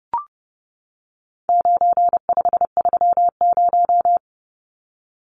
Morse code, heard as a keyed tone, about 30 wpm, 700 Hz. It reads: 9530
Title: Numbers - 4-Digit 30wpm